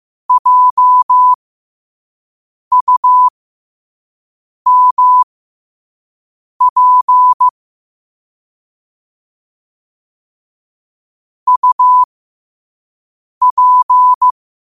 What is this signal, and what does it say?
Morse code, heard as a keyed tone, about 15 wpm, spaced as for 5 wpm, 1000 Hz. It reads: JUMP UP
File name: 5wpm